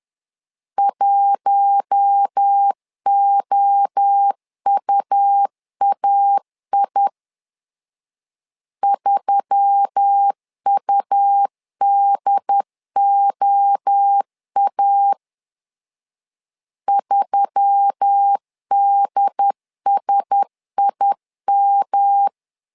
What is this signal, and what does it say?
Morse code, heard as a keyed tone, about 10 wpm, 800 Hz. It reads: 1OUAI 3UDOA 3DSIM
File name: LEZIONI / 4 Q1S3 10wpm 2427